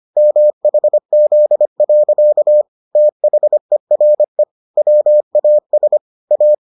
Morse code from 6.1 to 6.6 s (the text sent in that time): A